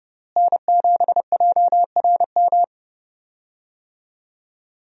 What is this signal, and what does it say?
Morse code, heard as a keyed tone, about 30 wpm, 700 Hz. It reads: N7JRM
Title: USA Call Signs - 5 Characters - Repeated 2x - 30wpm